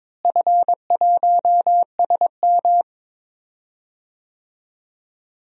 Morse code, heard as a keyed tone, about 21 wpm, 700 Hz. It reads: F1SM
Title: DXCC-100 Call Signs - 4 Characters - Repeated 3x - 22wpm